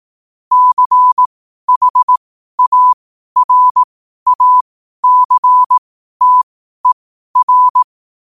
Morse code, heard as a keyed tone, about 18 wpm, spaced as for 12 wpm, 1000 Hz. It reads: CHARACTER
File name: qotdfiles / 12wpm